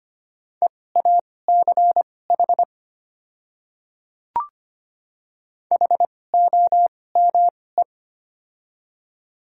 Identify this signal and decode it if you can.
Morse code, heard as a keyed tone, about 25 wpm, spaced as for 18 wpm, 700 Hz. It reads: EACH HOME
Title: 2x Character Spacing - Top 500 Words - 25wpm